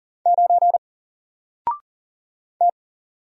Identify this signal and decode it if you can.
Morse code, heard as a keyed tone, about 40 wpm, 700 Hz. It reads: 9 T
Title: Single Character (Letter:Number:Punctuation:Prosign) 40wpm